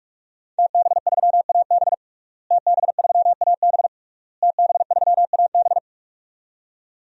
Morse code, heard as a keyed tone, about 41 wpm, 700 Hz. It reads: TB3AB TB3AB TB3AB
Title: DXCC-100 Call Signs - 5 Characters - Repeated 3x - 45wpm